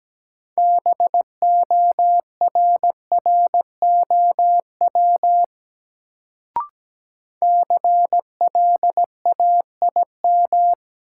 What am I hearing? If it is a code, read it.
Morse code, heard as a keyed tone, about 17 wpm, 700 Hz. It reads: BORROW CLAIM
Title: Sentences from Top 1500 Words - New Word Drill Alt - 17wpm